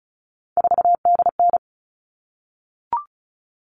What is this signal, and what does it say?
Morse code, heard as a keyed tone, about 34 wpm, 700 Hz. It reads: 4DN
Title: All Cut Numbers - 3-Digit 35wpm